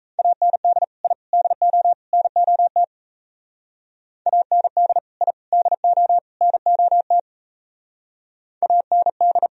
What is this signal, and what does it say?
Morse code, heard as a keyed tone, about 34 wpm, 700 Hz. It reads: AND I DO NOT AND I DO NOT AND
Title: Speed-Racing - Sets of 4 Words 28wpm